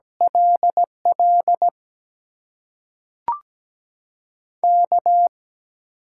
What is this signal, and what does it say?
Morse code, heard as a keyed tone, about 17 wpm, 700 Hz. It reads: LL K
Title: USA Call Signs - 6 Characters - 8x Character Spacing - 17wpm